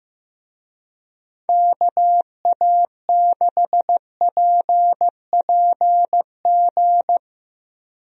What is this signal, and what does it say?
Morse code, heard as a keyed tone, about 15 wpm, 700 Hz. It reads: KA6PPG
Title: USA Call Signs - 6 Characters - Repeated 2x - 15wpm